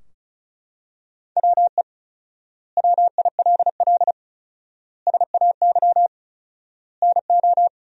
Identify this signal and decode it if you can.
Morse code, heard as a keyed tone, about 36 wpm, 700 Hz. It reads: WE WILL SAY NO
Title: 4x Word-Spacing - Sentences from Top 200 Words - 35wpm